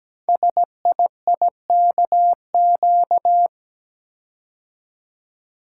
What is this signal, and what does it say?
Morse code, heard as a keyed tone, about 17 wpm, 700 Hz. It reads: SIIKQ
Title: Code Groups - 5 Letters - 17wpm